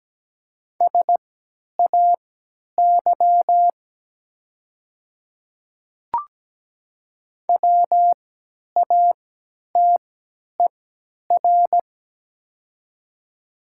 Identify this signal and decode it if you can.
Morse code, heard as a keyed tone, about 17 wpm, spaced as for 10 wpm, 700 Hz. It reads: SAY WATER
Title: 3x Character Spacing - Top 200 Words - 17wpm